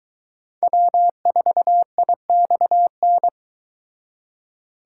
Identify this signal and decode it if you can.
Morse code, heard as a keyed tone, about 24 wpm, 700 Hz. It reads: W4IXN